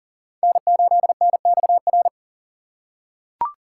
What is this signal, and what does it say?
Morse code, heard as a keyed tone, about 40 wpm, 700 Hz. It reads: N8NXR
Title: USA Call Signs - 6x Character Spacing - 40wpm